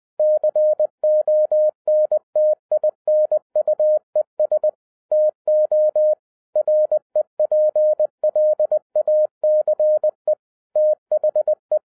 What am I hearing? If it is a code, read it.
Morse code, heard as a keyed tone, about 20 wpm, 610 Hz. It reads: CONTINUES TO REPLACE THE